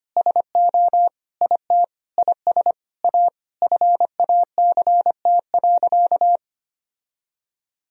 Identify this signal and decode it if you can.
Morse code, heard as a keyed tone, about 25 wpm, 700 Hz. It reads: SO IT IS A FACT.
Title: Sentences from Top 1000 Words - Review Part 3 - 25wpm